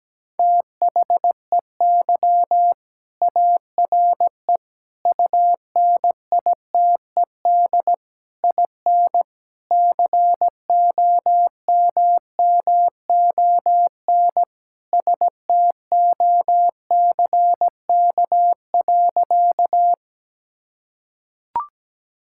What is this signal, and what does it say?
Morse code, heard as a keyed tone, about 17 wpm, 700 Hz. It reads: THEY ARE UNITED IN COMMON STOCK.
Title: Sentences from Top 1000 Words - Review Part 5 - 17wpm